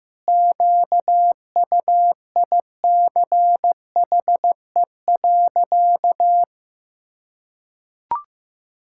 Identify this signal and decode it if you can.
Morse code, heard as a keyed tone, about 16 wpm, 700 Hz. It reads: QUICHE.
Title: Pangram Challenge v2 - 15wpm